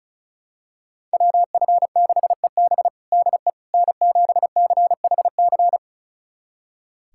Morse code, heard as a keyed tone, about 34 wpm, 700 Hz. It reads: WF6EB DE N7CHC
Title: USA Callsign DE Callsign - 5 Characters - 35wpm